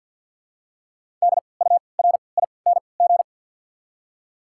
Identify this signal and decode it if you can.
Morse code, heard as a keyed tone, about 54 wpm, spaced as for 29 wpm, 690 Hz. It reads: DURING
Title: 3x Character Spacing - Top 500 Words - 50wpm